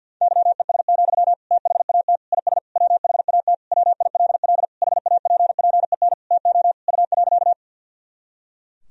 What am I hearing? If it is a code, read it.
Morse code, heard as a keyed tone, about 52 wpm, 700 Hz. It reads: YES, THAT IS WHAT WILL HAPPEN TO U.
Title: Sentences from Top 500 Words and Abbreviations 50wpm